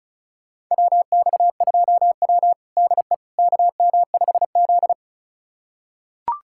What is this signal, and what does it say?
Morse code, heard as a keyed tone, about 34 wpm, 700 Hz. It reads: WX2W DE KM5Z
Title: USA Callsign DE Callsign - 4 Characters - 35wpm